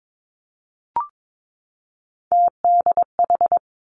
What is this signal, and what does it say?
Morse code, heard as a keyed tone, about 22 wpm, 700 Hz. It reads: TDH
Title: Three Challenging Letter-Number Combinations 22wpm